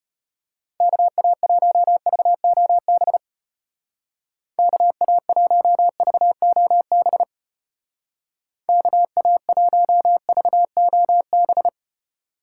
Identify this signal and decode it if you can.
Morse code, heard as a keyed tone, about 33 wpm, 700 Hz. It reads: KA1VOB KA1VOB KA1VOB